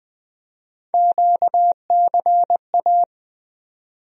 Morse code, heard as a keyed tone, about 20 wpm, 700 Hz. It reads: QCA
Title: USA Call Sign Postfixes - 3 Letters 20wpm